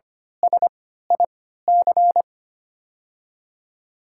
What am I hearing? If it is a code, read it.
Morse code, heard as a keyed tone, about 25 wpm, spaced as for 14 wpm, 700 Hz. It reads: SIC
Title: 3x Character Spacing - Top 300 Words - 25wpm